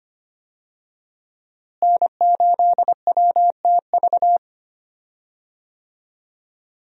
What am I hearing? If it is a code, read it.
Morse code, heard as a keyed tone, about 25 wpm, 700 Hz. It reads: N8WTV
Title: USA Call Signs - 5 Characters - Repeated 2x - 25wpm